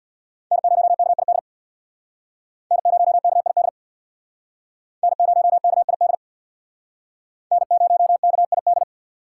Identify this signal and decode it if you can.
Morse code, heard as a keyed tone, about 61 wpm, 700 Hz. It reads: N0KID N0KID N0KID N0KID
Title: Speed Racing - USA Call Signs 50wpm